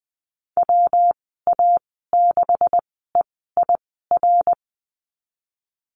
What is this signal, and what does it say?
Morse code, heard as a keyed tone, about 20 wpm, spaced as for 15 wpm, 700 Hz. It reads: WA6EIR
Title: 179 Lesson 29g - Slash - Call Signs 20wpm